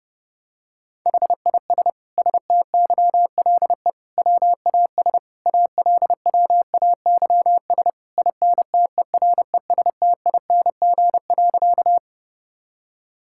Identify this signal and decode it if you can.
Morse code, heard as a keyed tone, about 32 wpm, 700 Hz. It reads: HIS STYLE WAS ALWAYS INTERESTING.
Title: Sentences from Top 1000 Words - Review Part 5 - 30wpm